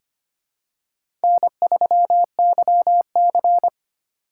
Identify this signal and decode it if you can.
Morse code, heard as a keyed tone, about 25 wpm, 700 Hz. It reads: N3YC
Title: USA Call Signs - 4 Characters - 3x Character Spacing - 25wpm